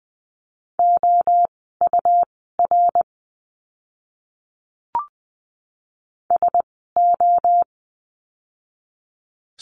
Morse code, heard as a keyed tone, about 20 wpm, spaced as for 14 wpm, 700 Hz. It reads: OUR SO
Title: 230 Lesson 37f - Letter x - Words 20wpm